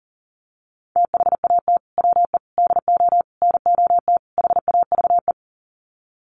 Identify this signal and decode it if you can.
Morse code, heard as a keyed tone, about 40 wpm, 700 Hz. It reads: THAT WE DO NOT HAVE
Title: Sets of 5 Words Encore 40wpm